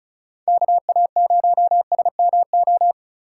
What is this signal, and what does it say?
Morse code, heard as a keyed tone, about 36 wpm, 700 Hz. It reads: KA0SMO
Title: USA Call Signs - 6 Characters - 6x Character Spacing - 35wpm